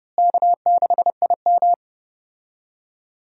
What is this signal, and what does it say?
Morse code, heard as a keyed tone, about 30 wpm, 700 Hz. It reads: K6IM
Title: USA Call Signs - 4 Characters - Repeated 3x - 30wpm